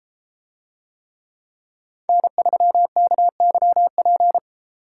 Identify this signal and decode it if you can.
Morse code, heard as a keyed tone, about 32 wpm, 700 Hz. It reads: N3KYP